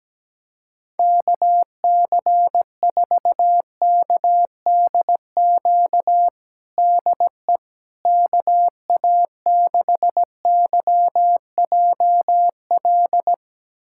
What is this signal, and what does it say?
Morse code, heard as a keyed tone, about 17 wpm, 700 Hz. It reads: KC4KDQ DE KA6YJL
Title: USA Callsign DE Callsign - 6 Characters - 17wpm